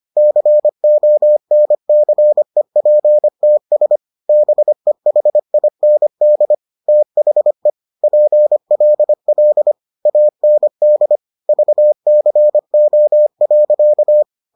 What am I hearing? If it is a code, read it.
Morse code, heard as a keyed tone, about 25 wpm, 600 Hz. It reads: CONCEPTS BEHIND THE PLL AND VCO.